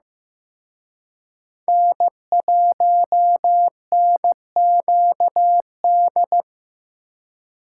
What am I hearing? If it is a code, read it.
Morse code, heard as a keyed tone, about 18 wpm, 700 Hz. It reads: N1NQD
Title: Speed Racing - USA 1x3 Call Signs 15wpm